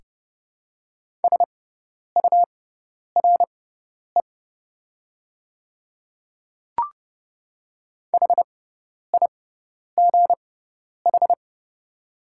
Morse code, heard as a keyed tone, about 29 wpm, spaced as for 10 wpm, 690 Hz. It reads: SURE HIGH
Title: 6x Character Spacing - Top 500 Words - 30wpm